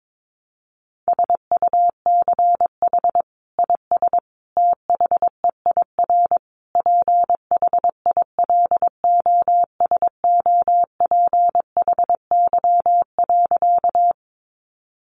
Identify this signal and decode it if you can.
Morse code, heard as a keyed tone, about 22 wpm, 700 Hz. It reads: SUCH IS THEIR PHILOSOPHY.
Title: Sentences from Top 900 Words 22wpm